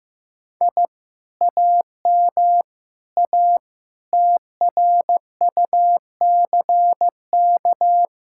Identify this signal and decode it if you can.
Morse code, heard as a keyed tone, about 15 wpm, 700 Hz. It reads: I AM A TRUCK
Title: Speed-Racing Common QSO Phrases Encore - 15wpm